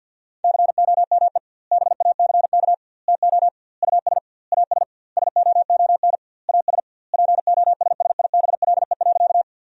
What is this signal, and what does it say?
Morse code, heard as a keyed tone, about 50 wpm, 700 Hz. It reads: COME BACK TO US AS SOON AS POSSIBLE.
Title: 2x Word-Spacing - Sentences from Top 500 Words - 50wpm